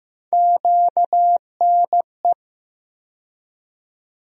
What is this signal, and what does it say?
Morse code, heard as a keyed tone, about 15 wpm, 700 Hz. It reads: QNE
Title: USA Call Sign Postfixes - 3 Letters 15wpm